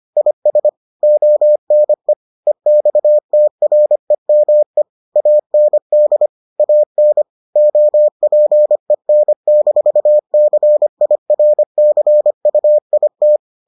Morse code, heard as a keyed tone, about 25 wpm, 600 Hz. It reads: IS ONE EXTREME AND AN OPEN-CIRCUIT